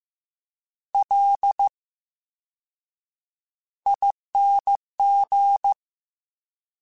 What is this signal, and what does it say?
Morse code, heard as a keyed tone, about 15 wpm, spaced as for 7 wpm, 800 Hz. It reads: LING
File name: Special INGs-mp3s / Special INGs-mp3s / VerySlowToFaster Ing @13